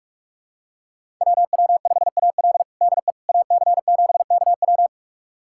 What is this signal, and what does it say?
Morse code, heard as a keyed tone, about 46 wpm, 700 Hz. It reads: WW5AL DE AC7KW
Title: USA Callsign DE Callsign - 5 Characters - 45wpm